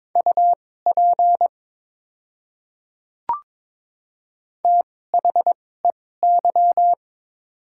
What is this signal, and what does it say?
Morse code, heard as a keyed tone, about 22 wpm, spaced as for 16 wpm, 700 Hz. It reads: UP THEY
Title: 2x Character Spacing - Top 100 Words - 22wpm